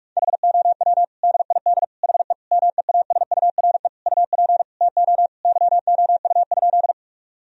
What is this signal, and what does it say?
Morse code, heard as a keyed tone, about 46 wpm, 700 Hz. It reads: HOW DID HE MEASURE UP TO YOU?
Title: Sentences from Top 1000 Words - Review Part 2 - 45wpm